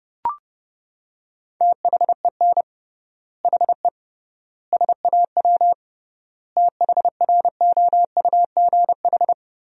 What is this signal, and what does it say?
Morse code, heard as a keyed tone, about 30 wpm, 700 Hz. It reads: THEN HE SAW THROUGH
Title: 3x Word-Spacing - Sentences from Top 400 Words - 30wpm